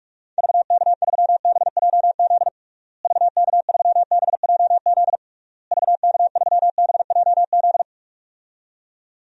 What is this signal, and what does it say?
Morse code, heard as a keyed tone, about 46 wpm, 700 Hz. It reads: VK3BJZ VK3BJZ VK3BJZ
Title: DXCC-100 Call Signs - 6 Characters - Repeated 3x - 45wpm